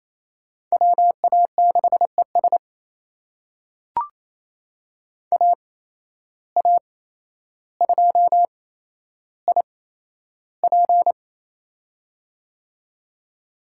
Morse code, heard as a keyed tone, about 28 wpm, spaced as for 12 wpm, 700 Hz. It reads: WA6ES AA2IP
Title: USA Call Signs - 5 Characters - 8x Character Spacing - 28wpm